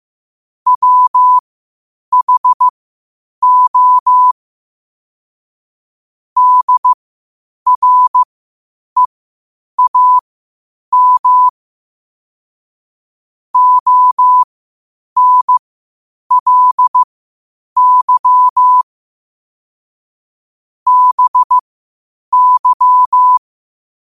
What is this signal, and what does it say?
Morse code, heard as a keyed tone, about 15 wpm, spaced as for 8 wpm, 1000 Hz. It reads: WHO DREAM ONLY BY
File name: qotdfiles / 8wpm